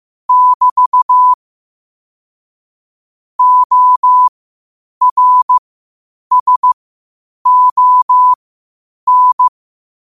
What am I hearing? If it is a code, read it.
Morse code, heard as a keyed tone, about 15 wpm, spaced as for 8 wpm, 1000 Hz. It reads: <BT> ORSON
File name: qotdfiles / 8wpm